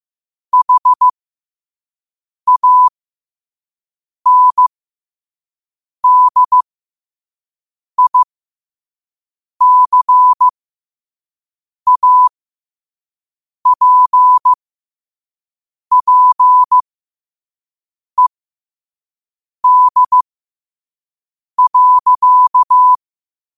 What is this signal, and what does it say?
Morse code, heard as a keyed tone, about 15 wpm, spaced as for 5 wpm, 1000 Hz. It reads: HANDICAPPED.
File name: 5wpm